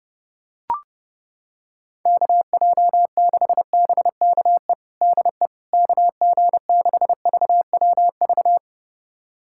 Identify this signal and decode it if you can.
Morse code, heard as a keyed tone, about 30 wpm, 700 Hz. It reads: KJ6BKE DE KG6VWV
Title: USA Callsign DE Callsign - 6 Characters - 30wpm